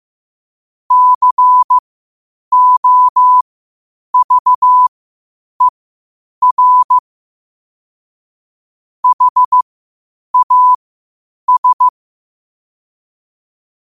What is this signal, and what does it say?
Morse code, heard as a keyed tone, about 15 wpm, spaced as for 8 wpm, 1000 Hz. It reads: COVER HAS
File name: qotdfiles / 8wpm